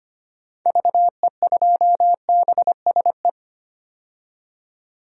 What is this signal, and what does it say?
Morse code, heard as a keyed tone, about 25 wpm, 700 Hz. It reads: VE2BSE
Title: DXCC-100 Call Signs - 6 Characters - 25wpm